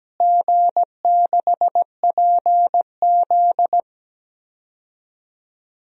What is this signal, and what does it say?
Morse code, heard as a keyed tone, about 17 wpm, 700 Hz. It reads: G6PZ
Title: DXCC-100 Call Signs - 4 Characters - Repeated 3x - 17wpm